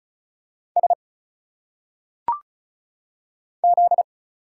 Morse code, heard as a keyed tone, about 35 wpm, 700 Hz. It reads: S Z
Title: Single Letters - Phonetic Pronunciation 35wpm